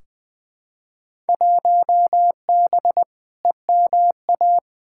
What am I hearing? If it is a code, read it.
Morse code, heard as a keyed tone, about 20 wpm, 700 Hz. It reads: 1B EMA
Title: Field Day Exchanges - 20wpm